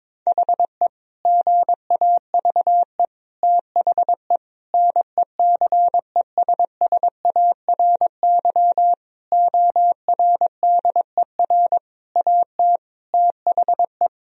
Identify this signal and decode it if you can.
Morse code, heard as a keyed tone, about 22 wpm, 700 Hz. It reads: HE GAVE THE NECESSARY ORDER AT THE
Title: Sentences from Top 1000 Words - Review Part 2 - 22wpm